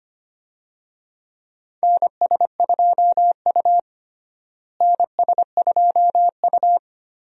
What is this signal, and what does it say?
Morse code, heard as a keyed tone, about 25 wpm, 700 Hz. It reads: NS2U NS2U
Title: USA Call Signs - 4 Characters - Repeated 2x - 25wpm